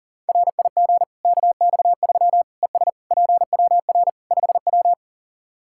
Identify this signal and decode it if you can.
Morse code, heard as a keyed tone, about 40 wpm, 700 Hz. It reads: RIG KX3 ES PWR 5W
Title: Practice QSO - Exchange 2 - Info - 40wpm